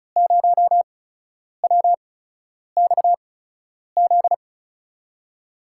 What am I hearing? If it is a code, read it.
Morse code, heard as a keyed tone, about 36 wpm, spaced as for 10 wpm, 700 Hz. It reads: 0WXZ
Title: USA Call Signs - 6 Characters - 8x Character Spacing - 35wpm